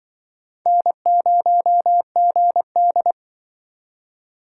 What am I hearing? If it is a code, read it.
Morse code, heard as a keyed tone, about 24 wpm, 700 Hz. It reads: N0GD
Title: Speed Racing - USA 1x2 Call Signs 20wpm